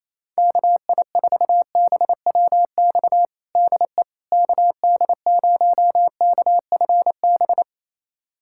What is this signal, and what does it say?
Morse code, heard as a keyed tone, about 28 wpm, 700 Hz. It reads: KI4BWX DE KD0KFB
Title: USA Callsign DE Callsign - 6 Characters - 28wpm